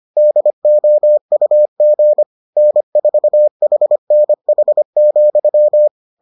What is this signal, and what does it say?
Morse code, heard as a keyed tone, about 25 wpm, 600 Hz. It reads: DOUG N4HNH,